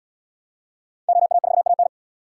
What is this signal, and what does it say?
Morse code, heard as a keyed tone, about 76 wpm, 700 Hz. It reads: OA4AN